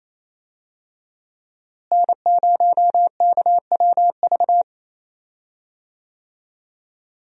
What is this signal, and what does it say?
Morse code, heard as a keyed tone, about 27 wpm, 700 Hz. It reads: N0KWV
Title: USA Call Signs - 5 Characters - Repeated 3x - 28wpm